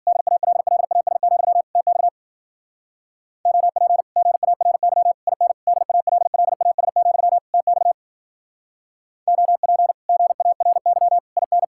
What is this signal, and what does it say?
Morse code, heard as a keyed tone, about 47 wpm, 700 Hz. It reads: DALLAS, TX OP GARY IN DALLAS, TX OP GARY IN
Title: Speed-Racing Common QSO Phrases Encore - 40wpm